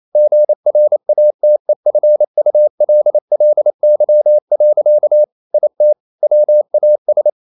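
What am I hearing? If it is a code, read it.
Morse code, heard as a keyed tone, about 28 wpm, 600 Hz. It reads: GRATEFULLY. IT WAS
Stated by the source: Morse code practice at 28 wpm, text of Ken Follett's novel Never in English, from CW med Gnister.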